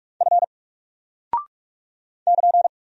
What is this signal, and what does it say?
Morse code, heard as a keyed tone, about 44 wpm, 700 Hz. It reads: F <KN>
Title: Single Character (Letter:Number:Punctuation:Prosign) 45wpm